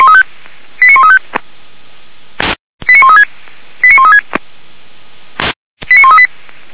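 Five-tone selective call (ZVEI), heard as quick runs of five tones, tones of about 70 ms. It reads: 89135 89136 89136 89138
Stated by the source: ZVEI five-tone selective call 89134 received on an SDR, recorded by feuerwehr-orlamuende.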